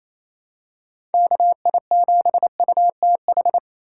Morse code, heard as a keyed tone, about 28 wpm, 700 Hz. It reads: KI7UTH
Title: USA Call Signs - 6 Characters - 6x Character Spacing - 28wpm